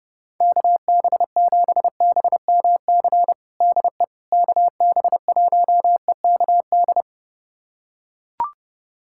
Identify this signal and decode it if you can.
Morse code, heard as a keyed tone, about 30 wpm, 700 Hz. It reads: KB7BMC DE KB1EKD
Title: USA Callsign DE Callsign - 6 Characters - 30wpm